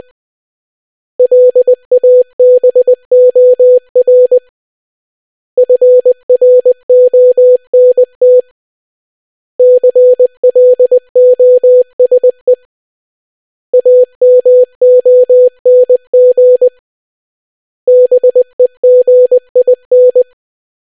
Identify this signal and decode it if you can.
Morse code, heard as a keyed tone, about 20 wpm, 500 Hz. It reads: LABOR FRONT CLOSE AMONG BEGIN